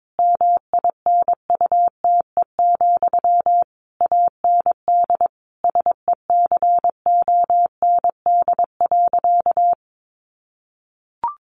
Morse code, heard as a keyed tone, about 22 wpm, 700 Hz. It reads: MINUTE, AND SECOND.
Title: A Sound Of Thunder 22wpm